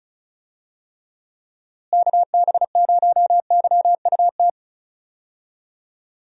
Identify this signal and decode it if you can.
Morse code, heard as a keyed tone, about 36 wpm, 700 Hz. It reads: KB0YUT
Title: USA Call Signs - 6 Characters - Repeated 2x - 35wpm